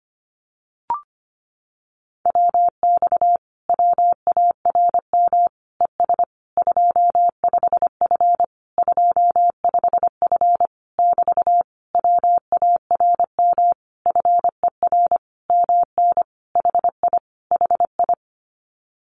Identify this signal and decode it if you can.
Morse code, heard as a keyed tone, about 25 wpm, 700 Hz. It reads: WX WARM ES 25F 25F <BT> WARM FER MN HI HI
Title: Common QSO Phrases Encore 25wpm